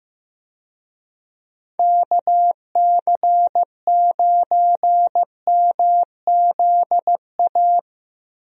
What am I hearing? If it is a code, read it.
Morse code, heard as a keyed tone, about 15 wpm, 700 Hz. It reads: KC9MZA
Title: USA Call Signs - 6 Characters - Repeated 3x - 15wpm